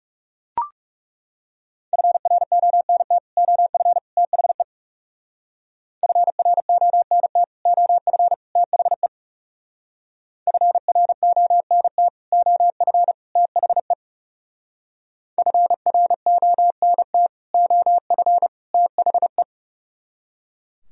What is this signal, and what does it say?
Morse code, heard as a keyed tone, about 37 wpm, 700 Hz. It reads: FRONT OF THE FRONT OF THE FRONT OF THE FRONT OF THE
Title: Speed-Racing - Sets of 3 Words 30wpm